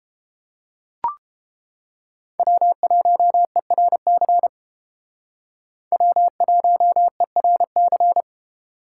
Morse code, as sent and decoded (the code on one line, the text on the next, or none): .-- .---- . .-. -.-. / .-- .---- . .-. -.-.
W1ERC W1ERC